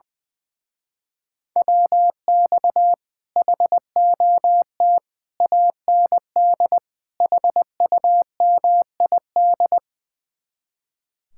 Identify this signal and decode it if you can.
Morse code, heard as a keyed tone, about 20 wpm, 700 Hz. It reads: WX HOT AND HUMID
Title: Speed-Racing Common QSO Phrases - 20wpm